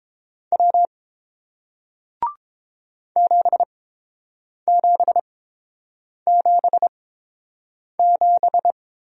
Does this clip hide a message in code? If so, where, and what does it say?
Yes: Morse code from 0.5 to 8.7 s, heard as a keyed tone, about 30 wpm, spaced as for 7 wpm, 700 Hz. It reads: W 7777